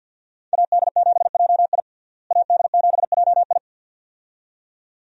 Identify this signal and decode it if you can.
Morse code, heard as a keyed tone, about 46 wpm, 700 Hz. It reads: AD7JI AD7JI
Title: USA Call Signs - 5 Characters - Repeated 2x - 50wpm